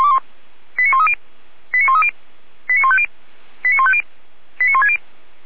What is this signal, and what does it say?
Five-tone selective call (ZVEI), heard as quick runs of five tones, tones of about 70 ms. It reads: 89130 89130 89150 89150 89160